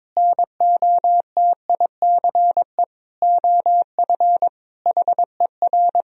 Morse code, heard as a keyed tone, about 22 wpm, 700 Hz. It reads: NOTICE OF HER
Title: Sentences from Top 1000 Words - Review Part 2 - 22wpm